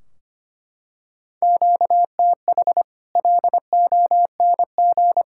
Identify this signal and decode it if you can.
Morse code, heard as a keyed tone, about 25 wpm, 700 Hz. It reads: QTH LONG
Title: Practice QSO - Exchange 1 - Beginning - 25wpm